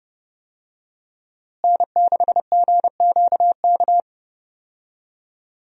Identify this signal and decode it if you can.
Morse code, heard as a keyed tone, about 30 wpm, 700 Hz. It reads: N6GQK
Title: USA Call Signs - 5 Characters - Repeated 2x - 30wpm